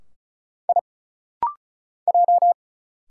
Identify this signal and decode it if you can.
Morse code, heard as a keyed tone, about 36 wpm, 700 Hz. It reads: I J
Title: Single Letter-Number Alt - Rapid-Fire 35wpm